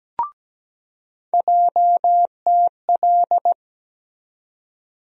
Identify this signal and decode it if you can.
Morse code, heard as a keyed tone, about 17 wpm, 700 Hz. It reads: JTL
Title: USA Call Sign Postfixes - 3 Letters 17wpm